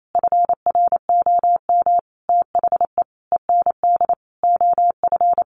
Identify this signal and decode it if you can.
Morse code, heard as a keyed tone, about 28 wpm, 700 Hz. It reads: FROM THE END OF
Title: Sentences from Top 1300 Words 28wpm